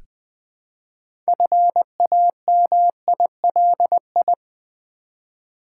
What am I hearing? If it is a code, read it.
Morse code, heard as a keyed tone, about 20 wpm, 700 Hz. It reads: FAMILI
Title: Build A Word - 3 Letters At A Time - 8-Letter Words - 20wpm